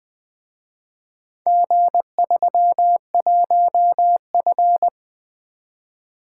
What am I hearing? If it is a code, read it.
Morse code, heard as a keyed tone, about 23 wpm, 700 Hz. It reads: G31F
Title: Speed-Racing - Four Letter-Number Combinations - 20wpm